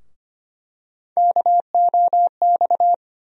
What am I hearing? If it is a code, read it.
Morse code, heard as a keyed tone, about 25 wpm, 700 Hz. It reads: KOX
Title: USA Call Sign Postfixes - 3 Letters 25wpm